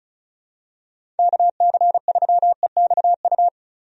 Morse code, heard as a keyed tone, about 36 wpm, 700 Hz. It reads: KC3EXU
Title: USA Call Signs - 6 Characters - Repeated 2x - 35wpm